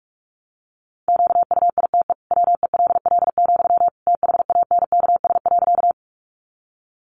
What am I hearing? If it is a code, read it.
Morse code, heard as a keyed tone, about 44 wpm, 700 Hz. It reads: QUITE WELL, THANKS.
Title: Sentences from Top 700 Words 45wpm